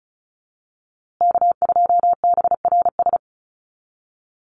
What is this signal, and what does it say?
Morse code, heard as a keyed tone, about 34 wpm, 700 Hz. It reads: K2BRS
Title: USA Call Signs Repeated 2x 35wpm